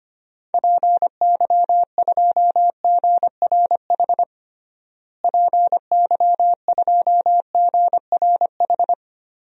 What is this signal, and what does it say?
Morse code, heard as a keyed tone, about 25 wpm, 700 Hz. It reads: PY2GRH PY2GRH
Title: DXCC-100 Call Signs - 6 Characters - Repeated 2x - 25wpm